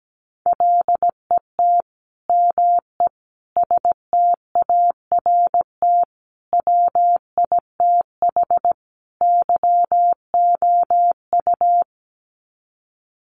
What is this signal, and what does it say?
Morse code, heard as a keyed tone, about 17 wpm, 700 Hz. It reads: LET ME START WITH YOU
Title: Sets of 5 Words Encore 17wpm